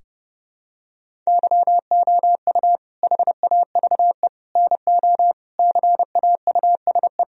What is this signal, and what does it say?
Morse code, heard as a keyed tone, about 32 wpm, 700 Hz. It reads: YOU HAVE NO CAUSE
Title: Sentences from Top 1000 Words - Review Part 1 - 30wpm